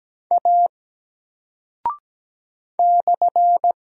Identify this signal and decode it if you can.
Morse code, heard as a keyed tone, about 17 wpm, 700 Hz. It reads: A /
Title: Single Character (Letter:Number:Punctuation:Prosign) 17wpm